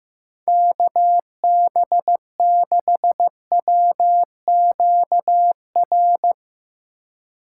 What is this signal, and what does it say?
Morse code, heard as a keyed tone, about 16 wpm, 700 Hz. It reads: KB6WQR
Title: Speed Racing - USA Call Signs 15wpm